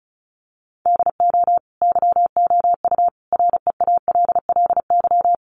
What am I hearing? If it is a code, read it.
Morse code, heard as a keyed tone, about 34 wpm, 700 Hz. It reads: DO YOU REALLY
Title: Sentences from Top 600 Words 35wpm